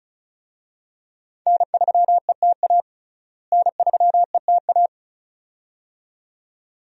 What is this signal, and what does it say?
Morse code, heard as a keyed tone, about 34 wpm, 700 Hz. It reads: N3ETA N3ETA
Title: USA Call Signs - 5 Characters - Repeated 2x - 35wpm